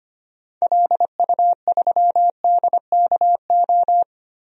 Morse code, heard as a keyed tone, about 25 wpm, 700 Hz. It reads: LU3DKO
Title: DXCC-100 Call Signs - 6 Characters - 25wpm